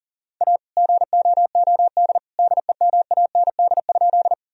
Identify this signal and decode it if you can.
Morse code, heard as a keyed tone, about 37 wpm, 700 Hz. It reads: A GOOD DEMAND?
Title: Sentences from Top 1000 Words - Review Part 4 - 40wpm